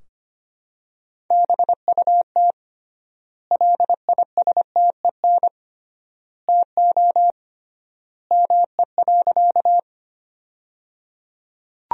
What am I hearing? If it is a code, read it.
Morse code, heard as a keyed tone, about 25 wpm, 700 Hz. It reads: BUT LISTEN TO ME.
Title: 3x Word-Spacing - Sentences from Top 400 Words - 25wpm